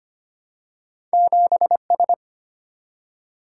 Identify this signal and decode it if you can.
Morse code, heard as a keyed tone, about 25 wpm, 700 Hz. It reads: 7S
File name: Two Letter-Number Combinations - 25wpm